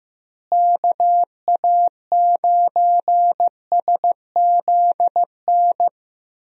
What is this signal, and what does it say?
Morse code, heard as a keyed tone, about 15 wpm, 700 Hz. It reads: KA9SZN
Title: USA Call Signs - 6 Characters - Repeated 2x - 15wpm